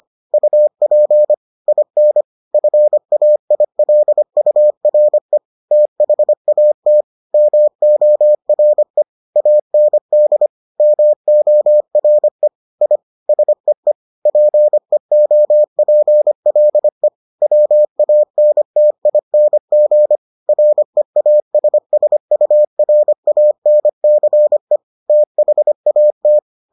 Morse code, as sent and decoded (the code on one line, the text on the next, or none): ..- .--. / .. -. / ..-. .- .. .-.. ..- .-. / - .... .- - / -- --- .-. / .- -. -.. / -- --- .-. / .. / ... . / .--. --- .--. .-.. / .-- .- -. - .. -. --. / .-. .- ... ... ..- .-. .- -. -.-. / - .... .- -
UP IN FAILURE THAT MORE AND MORE I SEE PEOPLE WANTING REASSURANCE THAT